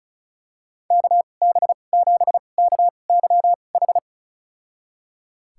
Morse code, heard as a keyed tone, about 34 wpm, spaced as for 25 wpm, 700 Hz. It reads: KB7KYH
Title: USA Call Signs - 2x Character Spacing - 35wpm